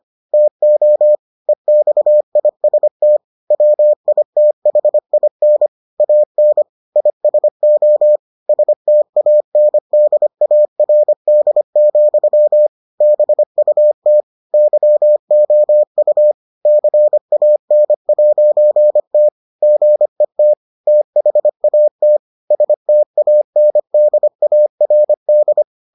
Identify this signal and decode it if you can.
Morse code, heard as a keyed tone, about 25 wpm, 600 Hz. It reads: TO EXIST WITHIN AN ISO STANDARD, BUT YOU CAN'T GET THAT STANDARD